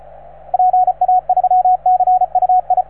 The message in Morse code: PA3CUI